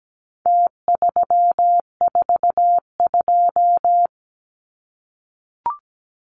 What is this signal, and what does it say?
Morse code, heard as a keyed tone, about 17 wpm, 700 Hz. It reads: T342
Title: Numbers - Contest Style - 4-Digit 17wpm